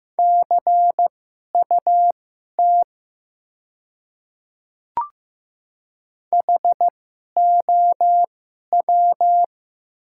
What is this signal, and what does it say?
Morse code, heard as a keyed tone, about 15 wpm, spaced as for 11 wpm, 700 Hz. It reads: CUT HOW
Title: 2x Character Spacing - Top 300 Words - 15wpm